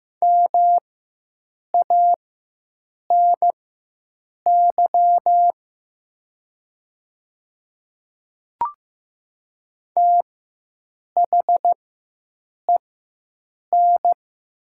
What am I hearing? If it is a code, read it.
Morse code, heard as a keyed tone, about 15 wpm, spaced as for 7 wpm, 700 Hz. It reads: MANY THEN
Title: 4x Character Spacing - Top 300 Words - 15wpm